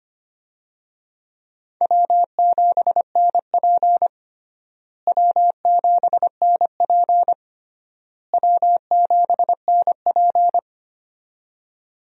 Morse code, heard as a keyed tone, about 25 wpm, 700 Hz. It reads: W7NP W7NP W7NP
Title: USA Call Signs - 4 Characters - Repeated 3x - 25wpm